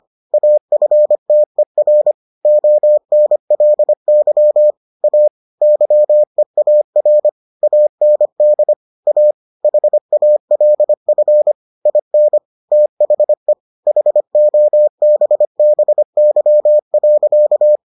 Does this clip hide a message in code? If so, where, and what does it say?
Yes: Morse code from 0.3 to 17.8 s, heard as a keyed tone, about 25 wpm, 600 Hz. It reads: AFTER ONLY A YEAR AND A HALF IN THE HOBBY.